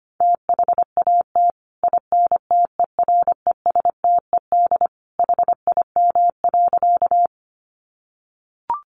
Morse code, heard as a keyed tone, about 25 wpm, 700 Hz. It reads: THAT INTERESTED HIM.
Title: Sentences from Top 900 Words 25wpm